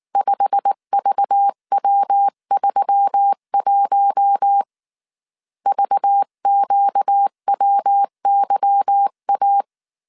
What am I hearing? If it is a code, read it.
Morse code, heard as a keyed tone, about 19 wpm, 800 Hz. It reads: 5VW31 VQWYA
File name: CW / DET / 1 9 1500C S C 18wpm 2539